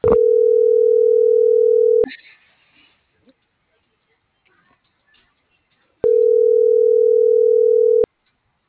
Ringback tone (North America), 440 + 480 Hz.